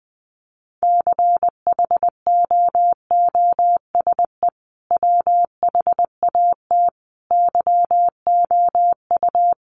Morse code, heard as a keyed tone, about 20 wpm, 700 Hz. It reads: CHOOSE WHAT YOU
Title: Sentences from Top 700 Words 20wpm